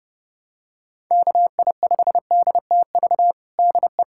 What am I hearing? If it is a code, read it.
Morse code, heard as a keyed tone, about 30 wpm, 700 Hz. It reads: KI5DTV DE
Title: USA Callsign DE Callsign - 6 Characters - 30wpm